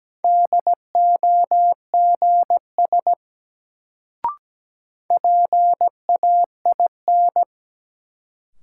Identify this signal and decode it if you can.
Morse code, heard as a keyed tone, about 17 wpm, 700 Hz. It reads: DOGS PAIN
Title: Sentences from Top 1600 Words - New Word Drill - 17wpm